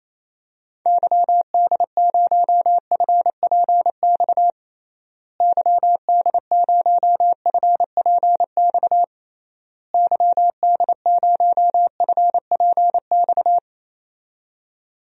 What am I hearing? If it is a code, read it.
Morse code, heard as a keyed tone, about 28 wpm, 700 Hz. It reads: YD0FPX YD0FPX YD0FPX
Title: DXCC-100 Call Signs - 6 Characters - Repeated 3x - 28wpm